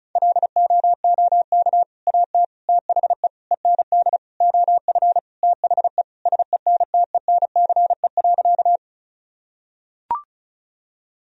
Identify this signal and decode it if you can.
Morse code, heard as a keyed tone, about 36 wpm, 700 Hz. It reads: LOOK AT THE END OF THE SENTENCE.
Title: Sentences from Top 1000 Words - Review Part 1 - 35wpm